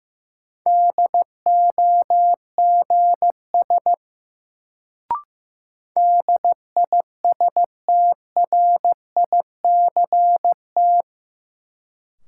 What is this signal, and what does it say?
Morse code, heard as a keyed tone, about 15 wpm, 700 Hz. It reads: DOGS DISTRICT
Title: Sentences from Top 1600 Words - New Word Drill Alt - 15wpm